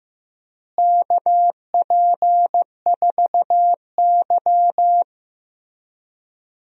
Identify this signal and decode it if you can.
Morse code, heard as a keyed tone, about 15 wpm, 700 Hz. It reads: KP4Y
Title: DXCC-100 Call Signs - 4 Characters - Repeated 3x - 15wpm